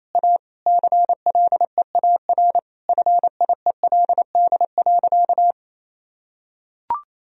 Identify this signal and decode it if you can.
Morse code, heard as a keyed tone, about 28 wpm, 700 Hz. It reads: A CLEAR FIELD.
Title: Sentences from Top 1000 Words - Review Part 2 - 28wpm